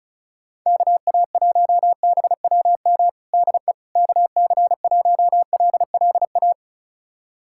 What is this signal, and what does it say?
Morse code, heard as a keyed tone, about 36 wpm, 700 Hz. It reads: KA1BWM DE KC1LLA
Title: USA Callsign DE Callsign 35wpm